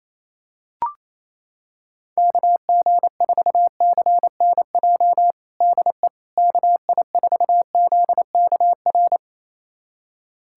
Morse code, heard as a keyed tone, about 28 wpm, 700 Hz. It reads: KG4CNJ DE KI4ZKR
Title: USA Callsign DE Callsign - 6 Characters - 28wpm